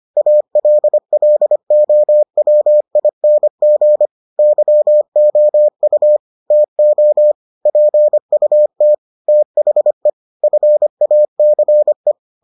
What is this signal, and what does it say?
Morse code, heard as a keyed tone, about 25 wpm, 600 Hz. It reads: ALLOWING YOU TO PUT THE FACE